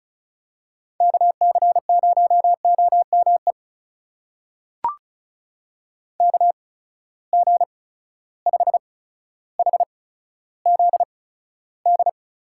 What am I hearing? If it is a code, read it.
Morse code, heard as a keyed tone, about 36 wpm, spaced as for 15 wpm, 700 Hz. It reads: KC0OME KG5HZD
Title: USA Call Signs - 6 Characters - 8x Character Spacing - 35wpm